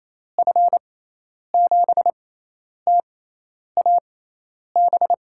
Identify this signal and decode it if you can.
Morse code, heard as a keyed tone, about 28 wpm, spaced as for 10 wpm, 700 Hz. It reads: F7TAB
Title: USA Call Signs - 6 Characters - 6x Character Spacing - 28wpm